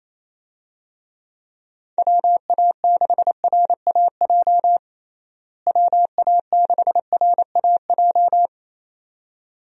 Morse code, heard as a keyed tone, about 27 wpm, 700 Hz. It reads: WA6RAJ WA6RAJ
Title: USA Call Signs - 6 Characters - Repeated 2x - 28wpm